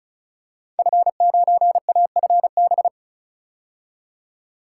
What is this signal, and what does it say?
Morse code, heard as a keyed tone, about 36 wpm, 700 Hz. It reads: F9AFB
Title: Code Groups - 5 Letter:Numbers - 35wpm